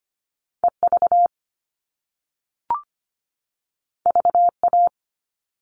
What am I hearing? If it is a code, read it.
Morse code, heard as a keyed tone, about 25 wpm, 700 Hz. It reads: EV VA
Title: All Cut Numbers - 2-Digit 25wpm